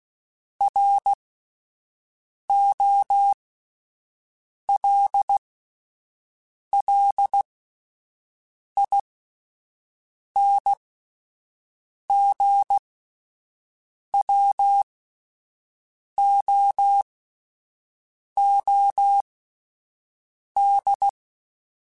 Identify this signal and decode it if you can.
Morse code, heard as a keyed tone, about 16 wpm, spaced as for 5 wpm, 800 Hz. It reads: ROLLINGWOOD